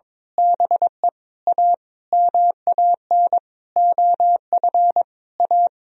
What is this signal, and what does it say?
Morse code, heard as a keyed tone, about 22 wpm, 700 Hz. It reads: BE A MAN OF A
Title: Sentences from Top 1000 Words - Review Part 3 - 22wpm